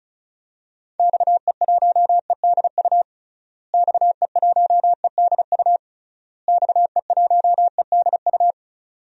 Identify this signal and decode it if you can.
Morse code, heard as a keyed tone, about 36 wpm, 700 Hz. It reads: XE1EDU XE1EDU XE1EDU
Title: DXCC-100 Call Signs - 6 Characters - Repeated 3x - 35wpm